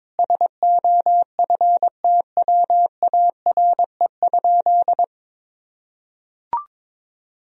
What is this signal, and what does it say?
Morse code, heard as a keyed tone, about 22 wpm, 700 Hz. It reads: SOFTWARE?
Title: Sentences from Top 1000 Words - Review Part 3 - 22wpm